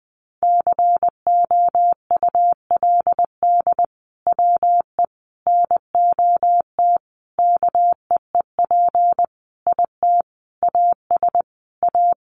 Morse code, heard as a keyed tone, about 20 wpm, 700 Hz. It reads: COULD WE NOT KEEP IT AS A
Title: Sentences from Top 1200 Words 20wpm